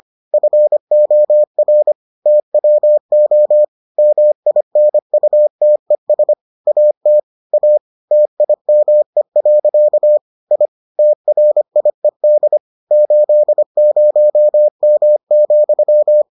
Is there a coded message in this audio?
Yes: Morse code from 0.3 to 16.2 s, heard as a keyed tone, about 25 wpm, 600 Hz. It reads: FOR TWO MINUTES AT A TIME. I TRIED 80M,